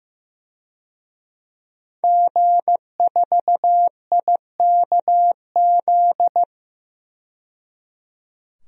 Morse code, heard as a keyed tone, about 15 wpm, 700 Hz. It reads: G4IKZ
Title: DXCC-100 Call Signs - 5 Characters - 15wpm